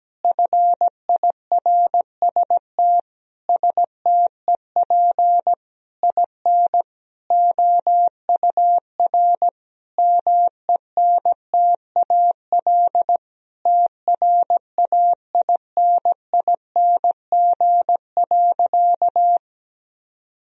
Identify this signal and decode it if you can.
Morse code, heard as a keyed tone, about 17 wpm, 700 Hz. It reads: FIRST STEP IN OUR MENTAL TRAINING.
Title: Sentences from Top 1000 Words - Review Part 4 - 17wpm